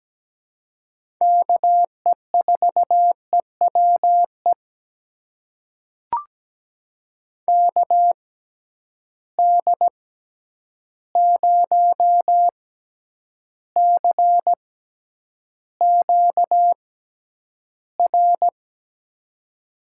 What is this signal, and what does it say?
Morse code, heard as a keyed tone, about 17 wpm, spaced as for 9 wpm, 700 Hz. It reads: KE4EWE KD0CQR
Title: USA Call Signs - 6 Characters - 6x Character Spacing - 17wpm